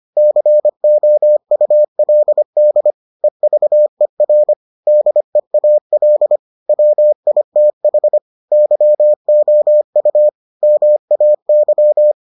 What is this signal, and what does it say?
Morse code, heard as a keyed tone, about 25 wpm, 600 Hz. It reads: COULD EVER DEAL WITH YOU MAY